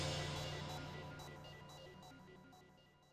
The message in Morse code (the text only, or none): none